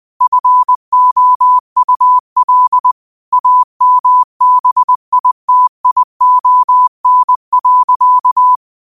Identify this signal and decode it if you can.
Morse code, heard as a keyed tone, about 20 wpm, 1000 Hz. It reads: FOUL AMBITION.
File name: qotdfiles / 20wpm